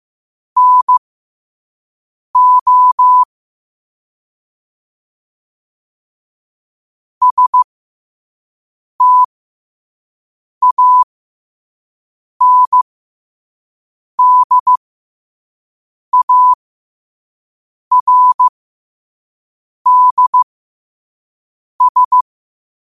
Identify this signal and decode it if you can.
Morse code, heard as a keyed tone, about 15 wpm, spaced as for 5 wpm, 1000 Hz. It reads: NO STANDARDS
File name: qotdfiles / 5wpm